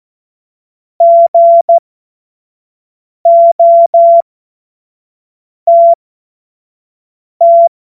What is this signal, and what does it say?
Morse code, heard as a keyed tone, about 14 wpm, spaced as for 5 wpm, 680 Hz. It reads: GOTT
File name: Morsekurs / Witze / Witz 01. August 2023 25 F70